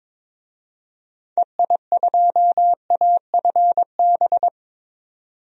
Morse code, heard as a keyed tone, about 22 wpm, 700 Hz. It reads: EI2AFB
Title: DXCC-100 Call Signs - 6 Characters - 22wpm